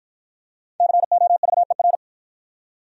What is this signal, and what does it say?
Morse code, heard as a keyed tone, about 52 wpm, 700 Hz. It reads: COVER